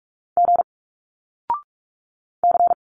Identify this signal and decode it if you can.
Morse code, heard as a keyed tone, about 46 wpm, 700 Hz. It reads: G C
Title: Single Letters 45wpm